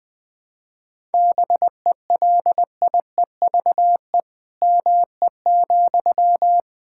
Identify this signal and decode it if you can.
Morse code, heard as a keyed tone, about 20 wpm, 700 Hz. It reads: BELIEVE ME,
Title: Sentences from Top 1000 Words - Review Part 6 - 20wpm